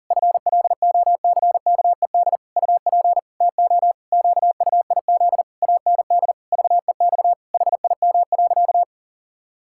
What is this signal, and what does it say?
Morse code, heard as a keyed tone, about 40 wpm, 700 Hz. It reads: FLOCKED UP TO QUIZ AND VEX HIM.
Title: Pangram Challenge v2 - 40wpm